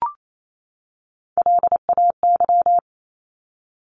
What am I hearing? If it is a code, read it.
Morse code, heard as a keyed tone, about 28 wpm, 700 Hz. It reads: LAY
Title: Top 500 Words 28wpm